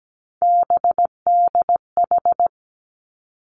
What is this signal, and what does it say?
Morse code, heard as a keyed tone, about 17 wpm, 700 Hz. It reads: BDH
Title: Three Challenging Letter-Number Combinations 17wpm